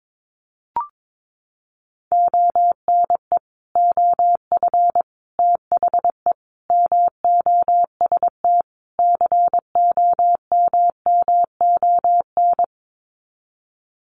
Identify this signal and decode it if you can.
Morse code, heard as a keyed tone, about 22 wpm, 700 Hz. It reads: ONE OF THE MOST COMMON
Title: Sets of 5 Words Encore2 22wpm